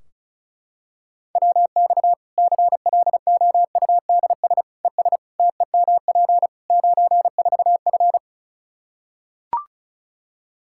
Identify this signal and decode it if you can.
Morse code, heard as a keyed tone, about 34 wpm, 700 Hz. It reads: WX CLOUDS ES TEMP 94F
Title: K1USN SST QSO Practice 35wpm